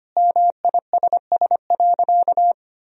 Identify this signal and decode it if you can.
Morse code, heard as a keyed tone, about 25 wpm, 700 Hz. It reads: MISS.
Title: Sentences from Top 1000 Words - Review Part 3 - 25wpm